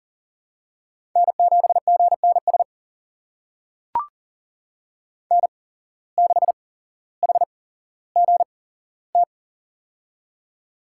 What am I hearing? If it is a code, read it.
Morse code, heard as a keyed tone, about 40 wpm, spaced as for 17 wpm, 700 Hz. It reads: N7GNS N6HGT
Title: USA Call Signs - 5 Characters - 8x Character Spacing - 40wpm